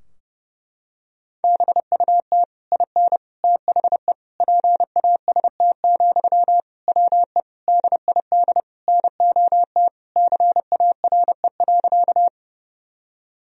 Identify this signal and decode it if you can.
Morse code, heard as a keyed tone, about 32 wpm, 700 Hz. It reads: BUT IN THE PAST, WE DID NOT CARE.
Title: Sentences from Top 1000 Words - Review Part 3 - 30wpm